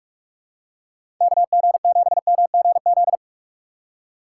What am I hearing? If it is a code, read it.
Morse code, heard as a keyed tone, about 46 wpm, 700 Hz. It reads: KG7MGZ
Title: USA Call Signs - 6 Characters - 8x Character Spacing - 45wpm